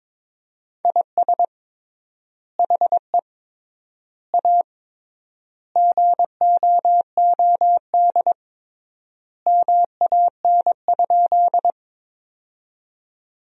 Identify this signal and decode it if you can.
Morse code, heard as a keyed tone, about 21 wpm, 700 Hz. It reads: IS HE A GOOD MAN?
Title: 3x Word-Spacing - Sentences from Top 200 Words - 22wpm